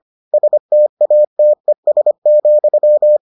Morse code, heard as a keyed tone, about 25 wpm, 600 Hz. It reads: STATES,